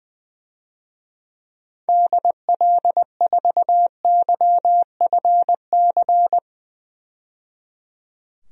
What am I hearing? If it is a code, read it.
Morse code, heard as a keyed tone, about 21 wpm, 700 Hz. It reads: DL4YFC
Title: DXCC-100 Call Signs - 6 Characters - 20wpm